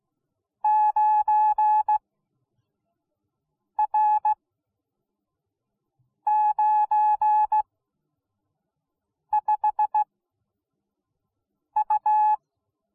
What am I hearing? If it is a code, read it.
Morse code, heard as a keyed tone, about 17 wpm, spaced as for 4 wpm, 850 Hz. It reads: 9R95U